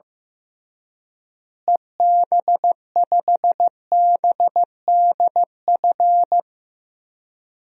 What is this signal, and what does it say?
Morse code, heard as a keyed tone, about 15 wpm, 700 Hz. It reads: EB5BDF
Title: DXCC-100 Call Signs - 6 Characters - Repeated 3x - 15wpm